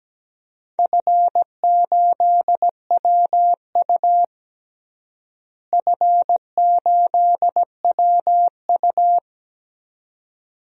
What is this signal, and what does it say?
Morse code, heard as a keyed tone, about 17 wpm, 700 Hz. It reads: F8WU F8WU
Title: DXCC-100 Call Signs - 4 Characters - Repeated 2x - 17wpm